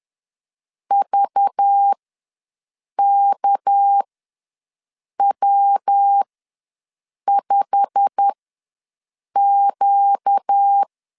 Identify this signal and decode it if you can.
Morse code, heard as a keyed tone, about 10 wpm, spaced as for 6 wpm, 800 Hz. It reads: VKW5Q
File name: LEZIONI / 9 2 ws 10wpm 4130